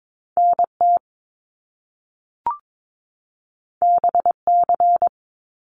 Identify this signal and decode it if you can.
Morse code, heard as a keyed tone, about 22 wpm, 700 Hz. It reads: NT BC
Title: Canadian Provinces and Territories 22wpm